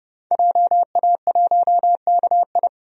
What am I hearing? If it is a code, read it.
Morse code, heard as a keyed tone, about 30 wpm, 700 Hz. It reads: JA1KI
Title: DXCC-100 Call Signs - 5 Characters - 30wpm